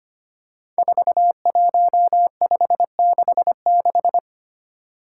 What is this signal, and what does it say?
Morse code, heard as a keyed tone, about 25 wpm, 700 Hz. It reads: 41566
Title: Five Challenging Letter-Number Combinations 25wpm